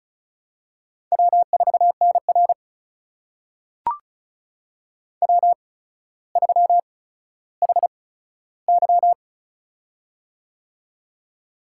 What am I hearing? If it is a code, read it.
Morse code, heard as a keyed tone, about 36 wpm, spaced as for 15 wpm, 700 Hz. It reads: W4NR W3HY
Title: USA Call Signs - 4 Characters - 8x Character Spacing - 35wpm